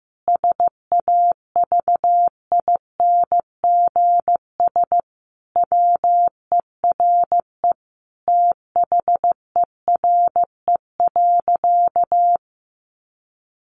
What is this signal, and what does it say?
Morse code, heard as a keyed tone, about 15 wpm, 700 Hz. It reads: SAVINGS WERE THERE.
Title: Sentences from Top 1200 Words 15wpm